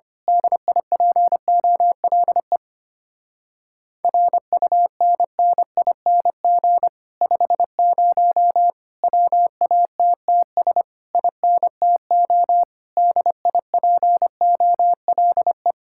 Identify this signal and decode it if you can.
Morse code, heard as a keyed tone, about 25 wpm, 700 Hz. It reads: DIPOLE RUNNING 50 WATTS INTO DIPOLE
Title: Speed-Racing Common QSO Phrases Encore - 25wpm